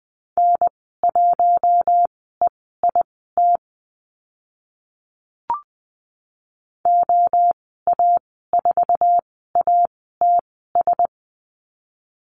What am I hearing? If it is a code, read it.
Morse code, heard as a keyed tone, about 20 wpm, spaced as for 15 wpm, 700 Hz. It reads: N1EIT OA4ATS
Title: 069 Lesson 12g - Letter d - Call Signs 20wpm